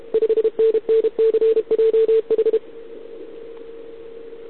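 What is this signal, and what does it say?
Morse code, heard as a keyed tone, about 28 wpm, 440 Hz. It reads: 5NNCJH
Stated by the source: Contest CW from OK1RI on 80 m during the OKOM DX contest, recorded by a QRP station using a KX3 and a wire antenna.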